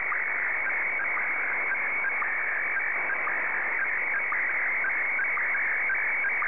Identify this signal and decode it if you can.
SSTV (slow-scan television), Scottie DX.
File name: sstv / SDX-VO5TSV